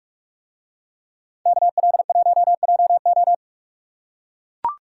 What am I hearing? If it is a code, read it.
Morse code, heard as a keyed tone, about 42 wpm, 700 Hz. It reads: KL1JO